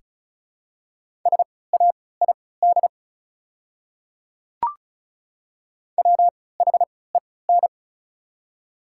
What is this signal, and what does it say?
Morse code, heard as a keyed tone, about 34 wpm, spaced as for 20 wpm, 700 Hz. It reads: SAID WHEN
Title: 3x Character Spacing - Top 100 Words - 35wpm